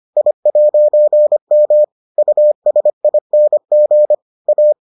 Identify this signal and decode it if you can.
Morse code, heard as a keyed tone, about 25 wpm, 600 Hz. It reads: I'M USING A